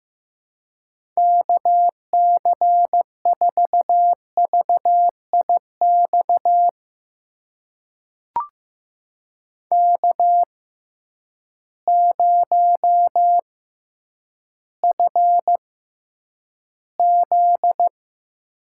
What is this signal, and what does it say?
Morse code, heard as a keyed tone, about 15 wpm, spaced as for 9 wpm, 700 Hz. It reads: KC4VIX K0FZ
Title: USA Call Signs - 6x Character Spacing - 15wpm